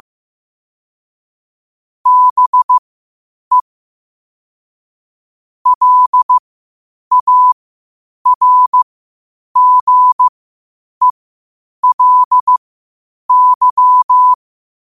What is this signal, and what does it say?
Morse code, heard as a keyed tone, about 15 wpm, spaced as for 8 wpm, 1000 Hz. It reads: BE LARGELY
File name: qotdfiles / 8wpm